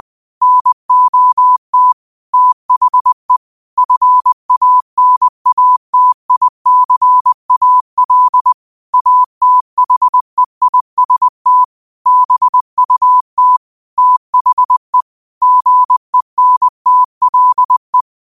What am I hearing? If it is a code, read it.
Morse code, heard as a keyed tone, about 20 wpm, 1000 Hz. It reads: NOT THE FANATICAL ATHEIST BUT THE GENTLE
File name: qotdfiles / 20wpm